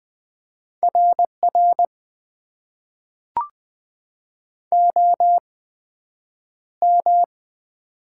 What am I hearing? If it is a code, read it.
Morse code, heard as a keyed tone, about 20 wpm, spaced as for 9 wpm, 700 Hz. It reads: RR OM
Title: Top 90 QSO Elements - 8x Character Spacing - 20wpm